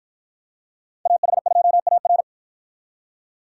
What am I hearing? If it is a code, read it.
Morse code, heard as a keyed tone, about 52 wpm, 700 Hz. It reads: AH2AR